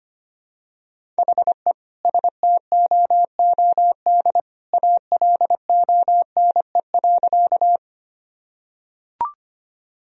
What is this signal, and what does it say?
Morse code, heard as a keyed tone, about 25 wpm, 700 Hz. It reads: HE STOOD ALONE.
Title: Sentences from Top 1000 Words - Review Part 2 - 25wpm